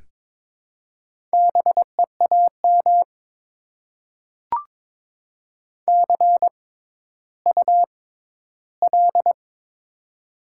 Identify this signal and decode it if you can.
Morse code, heard as a keyed tone, about 22 wpm, spaced as for 13 wpm, 700 Hz. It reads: BEAM CUL
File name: Top 90 QSO Elements - 6x Character Spacing - 22wpm